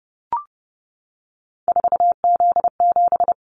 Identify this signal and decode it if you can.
Morse code, heard as a keyed tone, about 32 wpm, 700 Hz. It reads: 4Z7
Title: Callsign Prefixes for Top 50 DXCC - 30wpm